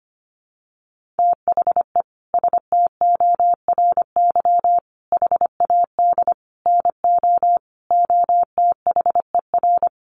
Morse code, heard as a keyed tone, about 25 wpm, 700 Hz. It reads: THE STORY HAD NO OTHER
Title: Sentences from Top 1200 Words 25wpm